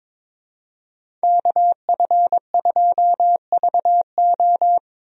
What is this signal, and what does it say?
Morse code, heard as a keyed tone, about 22 wpm, 700 Hz. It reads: KF2VO
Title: USA Call Signs - 5 Characters - Repeated 3x - 22wpm